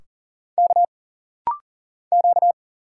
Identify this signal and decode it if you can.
Morse code, heard as a keyed tone, about 41 wpm, 700 Hz. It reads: K Q
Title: Single Letters - Rapid-Fire 40wpm